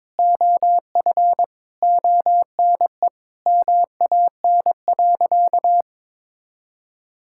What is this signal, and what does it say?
Morse code, heard as a keyed tone, about 22 wpm, 700 Hz. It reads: OF ONE MAN.
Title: Sentences from Top 1000 Words - Review Part 6 - 22wpm